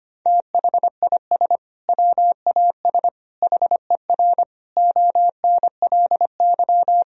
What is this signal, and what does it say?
Morse code, heard as a keyed tone, about 25 wpm, 700 Hz. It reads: THIS WAS HER ONLY
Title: Sentences from Top 1000 Words - Review Part 3 - 25wpm